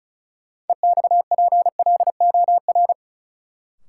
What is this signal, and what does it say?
Morse code, heard as a keyed tone, about 34 wpm, 700 Hz. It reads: EXPLOR
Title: Sentences from Top 1100 Words - New Word Drill Alt - 35wpm